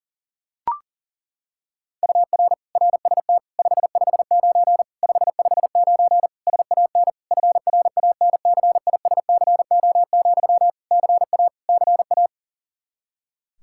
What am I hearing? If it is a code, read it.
Morse code, heard as a keyed tone, about 40 wpm, 700 Hz. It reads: UR RST 559 559 SAN FRANCISCO, CA CA
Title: Practice QSO - Exchange 1 - Beginning - 40wpm